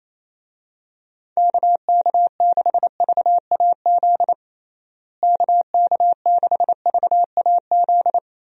Morse code, heard as a keyed tone, about 27 wpm, 700 Hz. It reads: KK6VAZ KK6VAZ
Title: USA Call Signs - 6 Characters - Repeated 3x - 28wpm